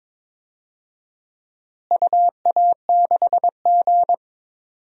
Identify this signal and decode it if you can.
Morse code, heard as a keyed tone, about 22 wpm, 700 Hz. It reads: UA6G
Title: DXCC-100 Call Signs - 4 Characters - Repeated 2x - 22wpm